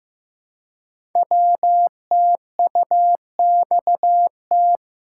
Morse code, heard as a keyed tone, about 15 wpm, 700 Hz. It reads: WTUXT
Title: Code Groups - 5 Letters - 15wpm